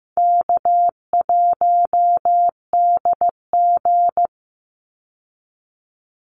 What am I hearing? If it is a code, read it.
Morse code, heard as a keyed tone, about 15 wpm, 700 Hz. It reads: K1DG
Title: Easy Call Signs Repeated 2x - 15wpm